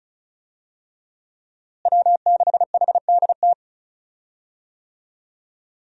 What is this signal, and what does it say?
Morse code, heard as a keyed tone, about 34 wpm, 700 Hz. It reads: W6HDT
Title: USA Call Signs - 5 Characters - Repeated 3x - 35wpm